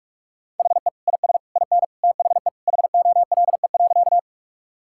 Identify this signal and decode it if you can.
Morse code, heard as a keyed tone, about 41 wpm, 700 Hz. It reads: HE IS IN THE HOLE.
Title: Sentences from Top 1600 Words 45wpm